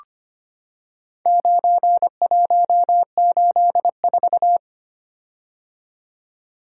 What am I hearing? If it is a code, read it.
Morse code, heard as a keyed tone, about 25 wpm, 700 Hz. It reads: 9184
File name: practice / 25wpm / Four Letter-Number Combinations - 25wpm